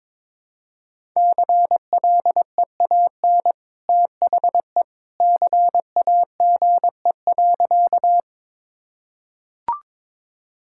Morse code, heard as a keyed tone, about 22 wpm, 700 Hz. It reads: CLEAN THE CAGE.
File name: practice / 22wpm / Easy Sentences with Hints - 22wpm